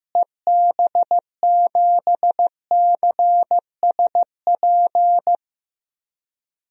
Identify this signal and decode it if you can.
Morse code, heard as a keyed tone, about 15 wpm, 700 Hz. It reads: EB7CSP
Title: DXCC-100 Call Signs - 6 Characters - Repeated 3x - 15wpm